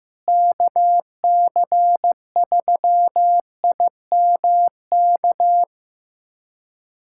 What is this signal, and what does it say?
Morse code, heard as a keyed tone, about 15 wpm, 700 Hz. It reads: KC3IMK
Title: USA Callsign DE Callsign - 6 Characters - 15wpm